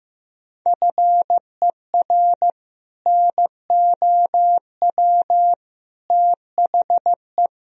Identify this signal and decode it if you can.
Morse code, heard as a keyed tone, about 15 wpm, 700 Hz. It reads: FER NOW THE
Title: Sentences from Top 200 Words and Abbreviations 15wpm